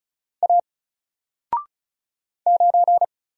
Morse code, heard as a keyed tone, about 36 wpm, 700 Hz. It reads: A 9
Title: Single Letter-Number Alt 35wpm